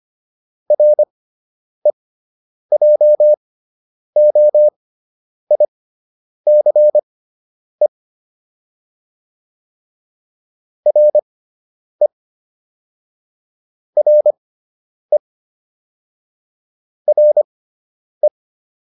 Morse code, heard as a keyed tone, about 25 wpm, spaced as for 9 wpm, 600 Hz. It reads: REJOICE RE RE RE